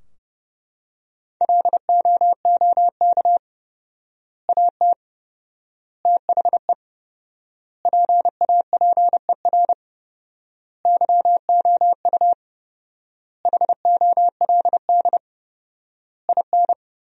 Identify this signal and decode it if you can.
Morse code, heard as a keyed tone, about 32 wpm, 700 Hz. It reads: LOOK AT THE PAPER YOU HOLD IN
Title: 4x Word-Spacing - Sentences from Top 400 Words - 30wpm